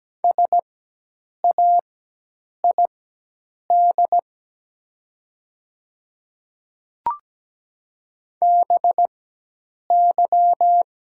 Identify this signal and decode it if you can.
Morse code, heard as a keyed tone, about 17 wpm, spaced as for 8 wpm, 700 Hz. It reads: SAID BY
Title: 4x Character Spacing - Top 100 Words - 17wpm